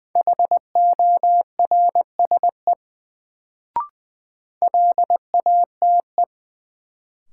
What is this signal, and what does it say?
Morse code, heard as a keyed tone, about 21 wpm, 700 Hz. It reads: HORSE LATE
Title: Sentences from Top 300 Words - New Word Drill - 20wpm